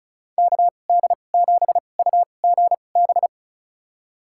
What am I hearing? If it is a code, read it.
Morse code, heard as a keyed tone, about 36 wpm, spaced as for 25 wpm, 700 Hz. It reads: KD7UGB
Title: USA Call Signs - 6 Characters - 2x Character Spacing - 35wpm